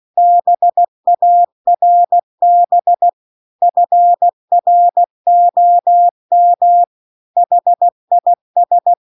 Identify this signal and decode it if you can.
Morse code, heard as a keyed tone, about 16 wpm, 700 Hz. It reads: BARB FROM HIS
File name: CW-The-Big-Twoheaded-River-2-16WPM